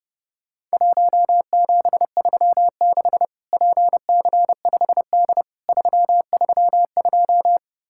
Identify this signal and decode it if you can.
Morse code, heard as a keyed tone, about 32 wpm, 700 Hz. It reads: 1736 PC5D 332
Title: Worked All Europe DX Contest - QTC Exchange - 30wpm